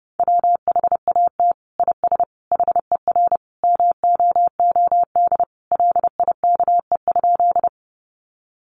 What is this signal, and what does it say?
Morse code, heard as a keyed tone, about 30 wpm, 700 Hz. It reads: WHAT IS HER MOOD LIKE?
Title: Sentences from Top 1100 Words 30wpm